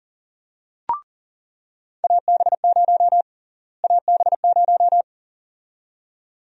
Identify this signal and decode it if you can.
Morse code, heard as a keyed tone, about 40 wpm, 700 Hz. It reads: AB0 AB0
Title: USA Call Sign Prefixes - Repeated 2x - 40wpm